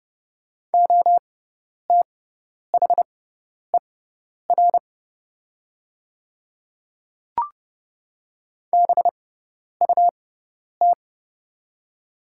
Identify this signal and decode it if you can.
Morse code, heard as a keyed tone, about 30 wpm, spaced as for 10 wpm, 700 Hz. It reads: OTHER BUT
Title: 6x Character Spacing - Top 100 Words - 30wpm